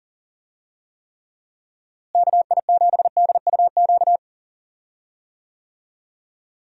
Morse code, heard as a keyed tone, about 40 wpm, 700 Hz. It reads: KI7DUQ
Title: USA Call Signs - 6 Characters - Repeated 3x - 40wpm